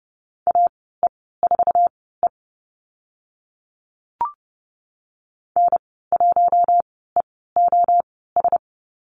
Morse code, heard as a keyed tone, about 30 wpm, spaced as for 17 wpm, 700 Hz. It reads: AE4E N1EOS
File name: Lesson 15g - Number 5 - Call Signs 30wpm